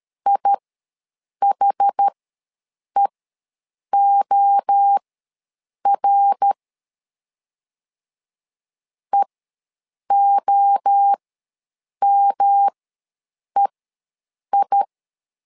Morse code, heard as a keyed tone, about 13 wpm, spaced as for 7 wpm, 800 Hz. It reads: IHEOR EOMEI
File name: LEZIONI / 3 MWRO / 3 MWRO ws 12wpm 1914